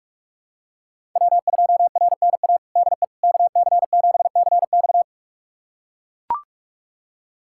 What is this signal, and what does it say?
Morse code, heard as a keyed tone, about 44 wpm, 700 Hz. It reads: W2RNA DE KC7CX
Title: USA Callsign DE Callsign - 5 Characters - 45wpm